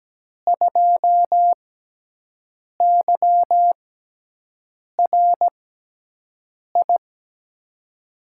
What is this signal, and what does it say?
Morse code, heard as a keyed tone, about 17 wpm, spaced as for 6 wpm, 700 Hz. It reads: 2YRI
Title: USA Call Signs - 6x Character Spacing - 17wpm